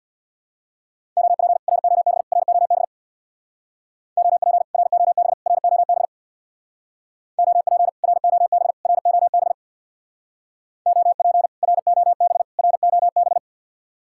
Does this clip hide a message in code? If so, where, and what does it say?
Yes: Morse code from 1.2 to 13.4 s, heard as a keyed tone, about 66 wpm, 700 Hz. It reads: OP ROB ROB OP ROB ROB OP ROB ROB OP ROB ROB